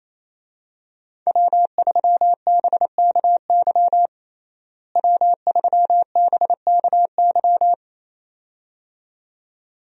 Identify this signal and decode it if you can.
Morse code, heard as a keyed tone, about 28 wpm, 700 Hz. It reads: W3BKY W3BKY
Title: USA Call Signs - 5 Characters - Repeated 2x - 28wpm